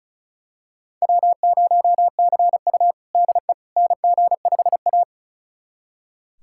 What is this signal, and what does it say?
Morse code, heard as a keyed tone, about 36 wpm, 700 Hz. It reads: W0CU DE NG5A
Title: USA Callsign DE Callsign - 4 Characters - 35wpm